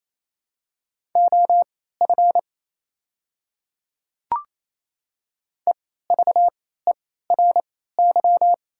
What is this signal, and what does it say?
Morse code, heard as a keyed tone, about 28 wpm, spaced as for 16 wpm, 700 Hz. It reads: OF EVERY
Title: 3x Character Spacing - Top 200 Words - 28wpm